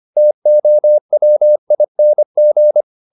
Morse code, heard as a keyed tone, about 25 wpm, 600 Hz. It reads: TOWING
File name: the old man and the sea part 1 25wpm